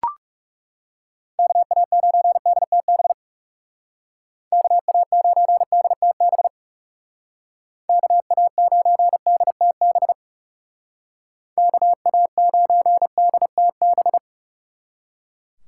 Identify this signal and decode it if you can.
Morse code, heard as a keyed tone, about 36 wpm, 700 Hz. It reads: KA9DTB KA9DTB KA9DTB KA9DTB
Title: Speed Racing - USA 2x3 Call Signs 30wpm